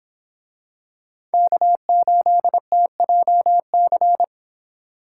Morse code, heard as a keyed tone, about 26 wpm, 700 Hz. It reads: K8TJC